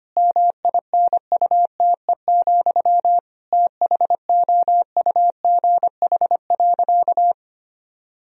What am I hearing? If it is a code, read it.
Morse code, heard as a keyed tone, about 25 wpm, 700 Hz. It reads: MINUTE, THOUGH.
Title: Sentences from Top 1000 Words - Review Part 2 - 25wpm